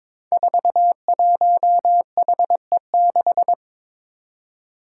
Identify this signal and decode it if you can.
Morse code, heard as a keyed tone, about 22 wpm, 700 Hz. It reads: 41HE6
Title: Five Challenging Letter-Number Combinations 22wpm